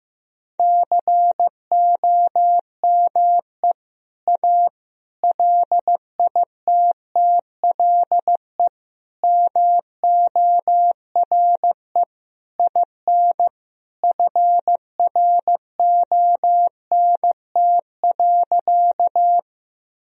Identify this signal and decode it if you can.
Morse code, heard as a keyed tone, about 15 wpm, 700 Hz. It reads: COME A LITTLE MORE IN FRONT.
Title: Sentences from Top 1000 Words - Review Part 2 - 15wpm